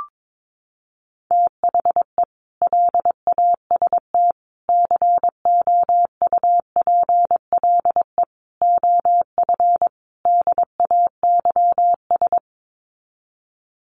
Morse code, heard as a keyed tone, about 22 wpm, 700 Hz. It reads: THE LAST COUPLE OF DAYS
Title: Sets of 5 Words Encore 22wpm